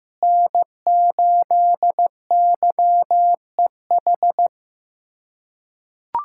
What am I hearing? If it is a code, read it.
Morse code, heard as a keyed tone, about 15 wpm, 700 Hz. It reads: N8YEH
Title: USA Callsign DE Callsign - 5 Characters - 15wpm